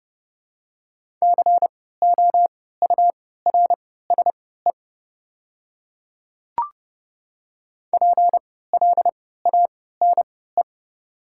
Morse code, heard as a keyed tone, about 30 wpm, spaced as for 17 wpm, 700 Hz. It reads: COURSE PLANE
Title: 3x Character Spacing - Top 500 Words - 30wpm